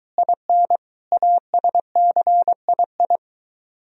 Morse code, heard as a keyed tone, about 23 wpm, 700 Hz. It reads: IN ASCII